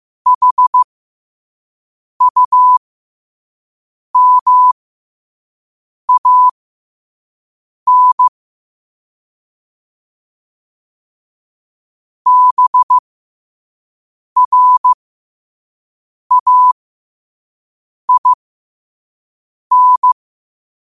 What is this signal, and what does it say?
Morse code, heard as a keyed tone, about 15 wpm, spaced as for 5 wpm, 1000 Hz. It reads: HUMAN BRAIN
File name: 5wpm